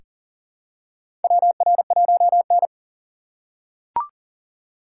Morse code, heard as a keyed tone, about 40 wpm, 700 Hz. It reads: WR1N
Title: USA Call Signs - 4 Characters - 6x Character Spacing - 40wpm